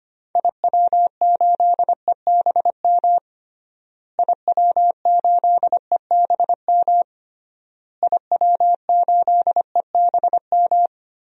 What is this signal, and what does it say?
Morse code, heard as a keyed tone, about 25 wpm, 700 Hz. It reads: IW8EBM IW8EBM IW8EBM
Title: DXCC-100 Call Signs - 6 Characters - Repeated 3x - 25wpm